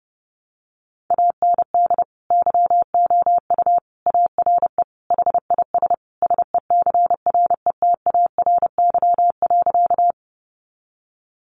Morse code, heard as a keyed tone, about 30 wpm, 700 Hz. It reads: AND YOU ARE HIS SECRETARY.
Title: Sentences from Top 1300 Words 30wpm